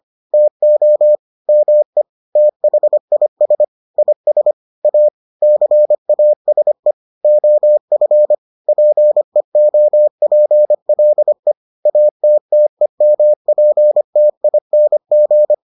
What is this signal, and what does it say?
Morse code, heard as a keyed tone, about 25 wpm, 600 Hz. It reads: TO ME THIS IS A CASE OF PEOPLE ATTEMPTING